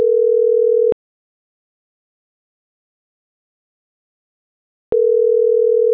Ringback tone (North America), 440 + 480 Hz.
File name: CENTRAL MAYAB